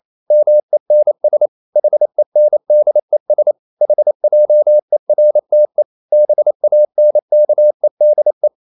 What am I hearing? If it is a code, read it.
Morse code, heard as a keyed tone, about 28 wpm, 600 Hz. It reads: MENS HENDES HJERTE BANKEDE
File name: Ferie QRQ 28wpm